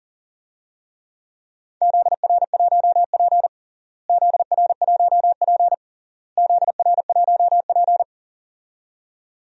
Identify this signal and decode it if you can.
Morse code, heard as a keyed tone, about 40 wpm, 700 Hz. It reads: ZR1P ZR1P ZR1P
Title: DXCC-100 Call Signs - 4 Characters - Repeated 3x - 40wpm